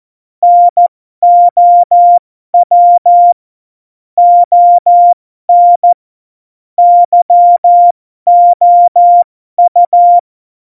Morse code, heard as a keyed tone, about 14 wpm, spaced as for 12 wpm, 700 Hz. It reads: NOW ON YOU
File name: Morsekurs / Witze